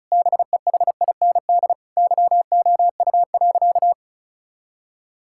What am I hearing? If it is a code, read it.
Morse code, heard as a keyed tone, about 34 wpm, 700 Hz. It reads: BEHIND YOU.
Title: Sentences from Top 1000 Words - Review Part 6 - 35wpm